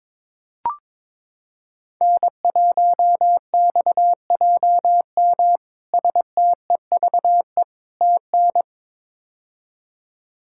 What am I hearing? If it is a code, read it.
Morse code, heard as a keyed tone, about 22 wpm, 700 Hz. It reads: N1XJM STEVE TN
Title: K1USN SST QSO Practice - 22wpm